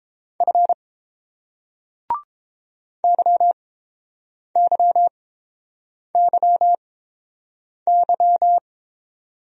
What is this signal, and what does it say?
Morse code, heard as a keyed tone, about 29 wpm, spaced as for 7 wpm, 700 Hz. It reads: F YYYY